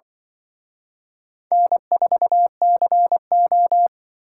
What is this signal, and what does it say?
Morse code, heard as a keyed tone, about 24 wpm, 700 Hz. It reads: N4CO
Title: Speed Racing - USA Call Signs 20wpm